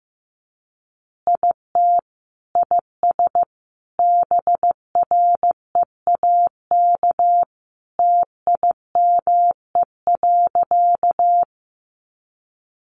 Morse code, heard as a keyed tone, about 15 wpm, 700 Hz. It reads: IT IS BREAK TIME.
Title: Easy Phrases 15wpm